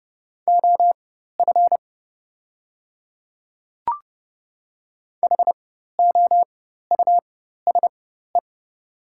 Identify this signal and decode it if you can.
Morse code, heard as a keyed tone, about 30 wpm, spaced as for 14 wpm, 700 Hz. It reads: OF HOUSE
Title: 4x Character Spacing - Top 200 Words - 30wpm